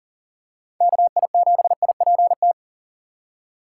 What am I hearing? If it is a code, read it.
Morse code, heard as a keyed tone, about 40 wpm, 700 Hz. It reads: KI7IPT
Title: USA Call Signs - 6 Characters - 6x Character Spacing - 40wpm